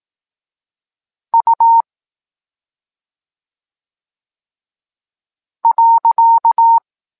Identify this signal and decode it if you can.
Morse code, heard as a keyed tone, about 18 wpm, 920 Hz. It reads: U .
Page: Tekst 14 – doskonalenie odbioru poznanych liter